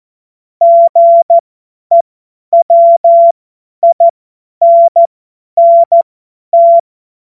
Morse code, heard as a keyed tone, about 14 wpm, spaced as for 10 wpm, 680 Hz. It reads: GEWINNT
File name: Morsekurs / Witze / Witz 09. August 2022 50 F70